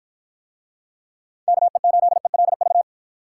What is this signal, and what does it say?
Morse code, heard as a keyed tone, about 52 wpm, 700 Hz. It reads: KE8ELV